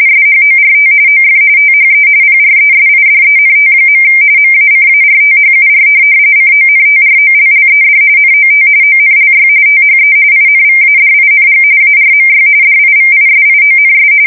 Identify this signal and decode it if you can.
RTTY, about 45.45 baud. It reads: SHORT OF THE PRESIDENT. MACARTHUR CONTROLLED NOT ONLY THE INVADING ARMY DIVISIONS
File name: fsk170 / USS-New-Jersey